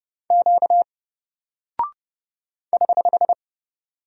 Morse code, heard as a keyed tone, about 30 wpm, 700 Hz. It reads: Q <HH>
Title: Single Character (Letter:Number:Punctuation:Prosign) Alt 30wpm